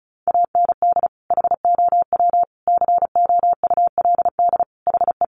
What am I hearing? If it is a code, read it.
Morse code, heard as a keyed tone, about 34 wpm, 700 Hz. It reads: AND HOW COULD HE
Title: Sentences from Top 1500 Words 35wpm